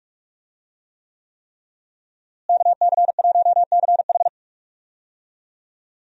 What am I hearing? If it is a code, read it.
Morse code, heard as a keyed tone, about 45 wpm, 700 Hz. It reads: KC1CH
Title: USA Call Signs - 5 Characters - Repeated 3x - 45wpm